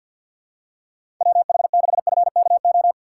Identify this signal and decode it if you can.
Morse code, heard as a keyed tone, about 51 wpm, 700 Hz. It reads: WH6FKO
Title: USA Call Signs - 6 Characters - 4x Character Spacing - 50wpm